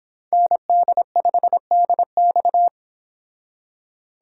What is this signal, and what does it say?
Morse code, heard as a keyed tone, about 25 wpm, 700 Hz. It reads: ND5DX